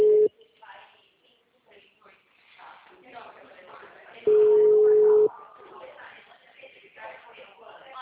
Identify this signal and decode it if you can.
Ringback tone (Europe, 425 Hz), with speech.
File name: out- -2001- - - .1266